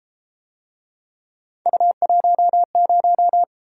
Morse code, heard as a keyed tone, about 32 wpm, 700 Hz. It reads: U10